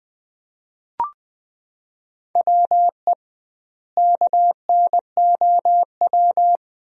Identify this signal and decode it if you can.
Morse code, heard as a keyed tone, about 21 wpm, 700 Hz. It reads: WE KNOW
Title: 2x Word-Spacing - Sentences from Top 500 Words - 20wpm